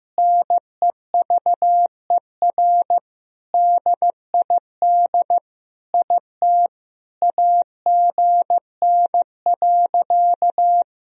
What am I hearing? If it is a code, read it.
Morse code, heard as a keyed tone, about 15 wpm, 700 Hz. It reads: NEVER DID IT AGN.
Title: Sentences from Top 300 Words and Abbreviations 15wpm